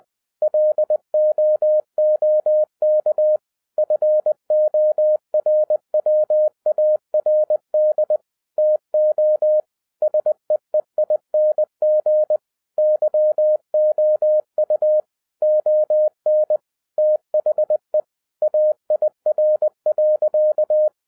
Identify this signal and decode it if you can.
Morse code, heard as a keyed tone, about 20 wpm, 610 Hz. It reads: LOOK FORWARD TO SEEING YOU ON THE AIR.
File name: 2012-01-mp3 / n16-THROUGH EIGHT MONTHS-20wpm